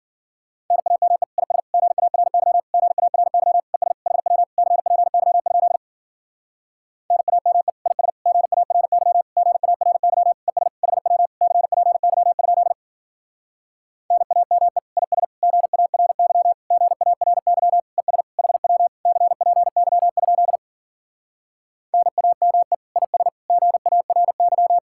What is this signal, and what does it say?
Morse code, heard as a keyed tone, about 52 wpm, 700 Hz. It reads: NAME IS GARY GARY ES HW CPY? NAME IS GARY GARY ES HW CPY? NAME IS GARY GARY ES HW CPY? NAME IS GARY